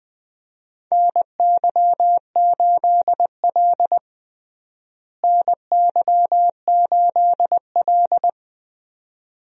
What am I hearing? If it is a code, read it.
Morse code, heard as a keyed tone, about 20 wpm, 700 Hz. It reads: NY8L NY8L
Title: USA Call Signs - 4 Characters - Repeated 2x - 20wpm